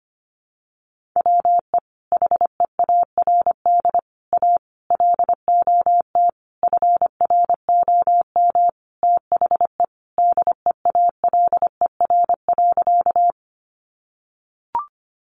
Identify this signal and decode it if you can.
Morse code, heard as a keyed tone, about 25 wpm, 700 Hz. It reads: WE HEARD A LOT FROM THE DEALER.
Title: Sentences from Top 1100 Words 25wpm